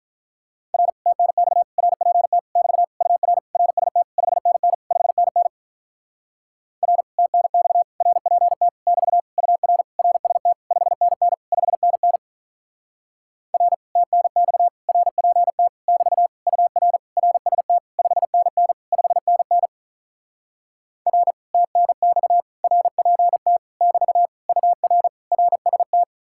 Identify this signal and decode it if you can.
Morse code, heard as a keyed tone, about 46 wpm, 700 Hz. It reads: R TNX RPT <BT> UR RST 5NN 5NN R TNX RPT <BT> UR RST 5NN 5NN R TNX RPT <BT> UR RST 5NN 5NN R TNX RPT <BT> UR RST